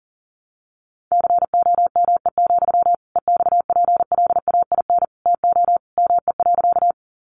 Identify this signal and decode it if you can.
Morse code, heard as a keyed tone, about 40 wpm, 700 Hz. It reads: COME, EXPLAIN TO ME.
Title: Sentences from Top 1100 Words 40wpm